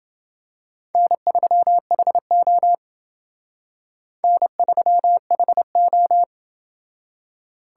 Morse code, heard as a keyed tone, about 28 wpm, 700 Hz. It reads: N3HO N3HO